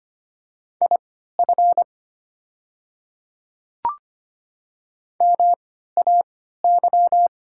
Morse code, heard as a keyed tone, about 25 wpm, spaced as for 14 wpm, 700 Hz. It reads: IF MAY
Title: 3x Character Spacing - Top 300 Words - 25wpm